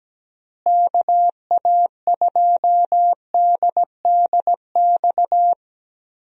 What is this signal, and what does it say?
Morse code, heard as a keyed tone, about 17 wpm, 700 Hz. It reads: KA2DDX
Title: USA Call Signs - 6 Characters - Repeated 2x - 17wpm